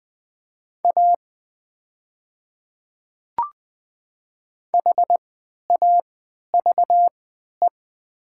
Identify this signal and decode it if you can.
Morse code, heard as a keyed tone, about 20 wpm, spaced as for 11 wpm, 700 Hz. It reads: A HAVE
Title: 3x Character Spacing - Top 200 Words - 20wpm